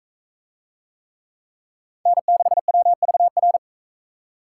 Morse code, heard as a keyed tone, about 41 wpm, 700 Hz. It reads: N6WVR